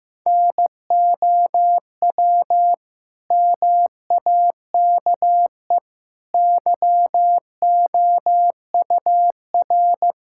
Morse code, heard as a keyed tone, about 15 wpm, 700 Hz. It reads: NOW MAKE YOUR
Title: Sentences from Top 1000 Words - Review Part 6 - 15wpm